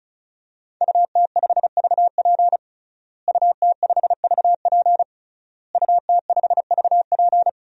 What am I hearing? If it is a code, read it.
Morse code, heard as a keyed tone, about 34 wpm, 700 Hz. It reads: UT5VP UT5VP UT5VP
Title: DXCC-100 Call Signs - 5 Characters - Repeated 3x - 35wpm